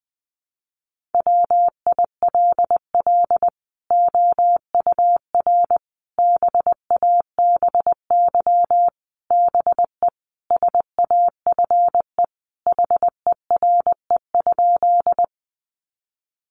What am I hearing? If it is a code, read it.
Morse code, heard as a keyed tone, about 20 wpm, 700 Hz. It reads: WILL OUR BABY BE SAFE HERE?
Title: Sentences from Top 1300 Words 20wpm